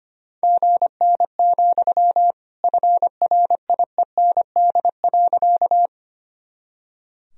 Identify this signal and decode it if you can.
Morse code, heard as a keyed tone, about 25 wpm, 700 Hz. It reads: GN, FRIEND.
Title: Sentences from Top 300 Words and Abbreviations 25wpm